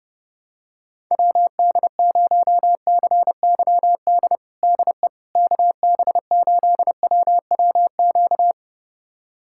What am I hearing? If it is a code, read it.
Morse code, heard as a keyed tone, about 30 wpm, 700 Hz. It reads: WD0CYD DE KB8WWQ
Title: USA Callsign DE Callsign - 6 Characters - 30wpm